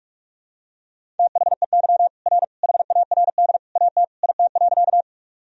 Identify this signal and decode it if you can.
Morse code, heard as a keyed tone, about 45 wpm, 700 Hz. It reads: THEY R HARD AT IT.
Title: Sentences from Top 300 Words and Abbreviations 45wpm